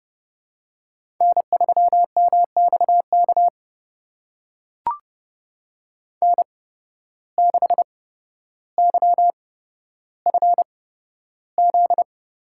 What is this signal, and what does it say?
Morse code, heard as a keyed tone, about 31 wpm, spaced as for 13 wpm, 700 Hz. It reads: N3MXK N6YFZ
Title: USA Call Signs - 5 Characters - 8x Character Spacing - 30wpm